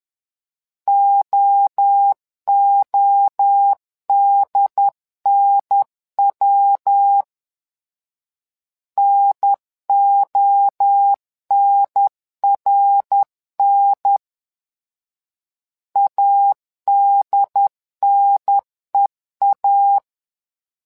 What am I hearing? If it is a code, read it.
Morse code, heard as a keyed tone, about 11 wpm, 800 Hz. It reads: OODNW NONRN ADNEA
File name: LEZIONI / 3 MWRO / 3 MWRO 10wpm 2148